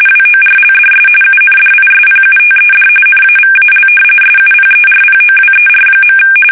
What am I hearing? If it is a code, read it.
RTTY, about 45.45 baud. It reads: WORLDS UNDISPUTED NAVAL SUPERPOWER.
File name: fsk850 / usnavy5